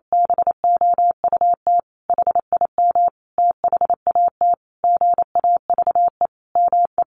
Morse code, heard as a keyed tone, about 28 wpm, 700 Hz. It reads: BOUT HIM THAT GAVE ME
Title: Sentences from Top 1500 Words 28wpm